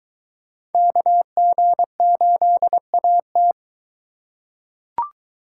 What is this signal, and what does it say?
Morse code, heard as a keyed tone, about 21 wpm, 700 Hz. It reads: KG8AT